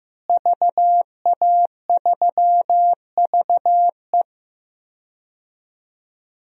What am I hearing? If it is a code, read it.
Morse code, heard as a keyed tone, about 15 wpm, 700 Hz. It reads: VA3VE
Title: DXCC-100 Call Signs - 5 Characters - Repeated 3x - 15wpm